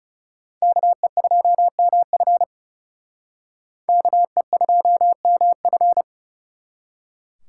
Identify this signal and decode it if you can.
Morse code, heard as a keyed tone, about 36 wpm, 700 Hz. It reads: KE2MF KE2MF
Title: Speed Racing - USA 2x2 Call Signs 30wpm